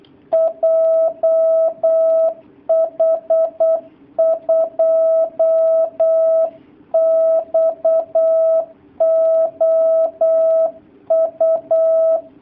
Morse code, heard as a keyed tone, about 8 wpm, 650 Hz. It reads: JH2XOU
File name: jh8cbh / 2019cw / sp08- w